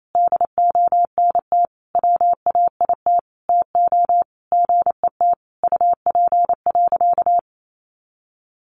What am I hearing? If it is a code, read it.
Morse code, heard as a keyed tone, about 28 wpm, 700 Hz. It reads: DONT WAIT TO GET UP.
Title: Sentences from Top 500 Words 28wpm